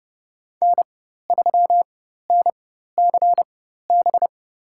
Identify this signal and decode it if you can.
Morse code, heard as a keyed tone, about 30 wpm, spaced as for 14 wpm, 700 Hz. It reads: N3NCB
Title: USA Call Signs - 5 Characters - 4x Character Spacing - 30wpm